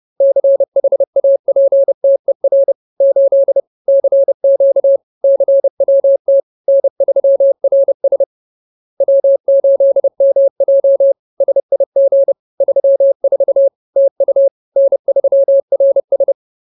Morse code, heard as a keyed tone, about 30 wpm, 550 Hz. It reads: CHAPTER 8 CQ CWT N3RS W8MJ SIG 34 TU N3RS